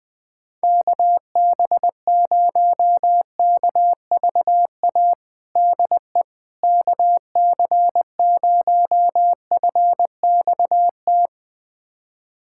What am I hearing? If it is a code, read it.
Morse code, heard as a keyed tone, about 21 wpm, 700 Hz. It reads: KB0KVA DE KC0FXT
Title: USA Callsign DE Callsign - 6 Characters - 20wpm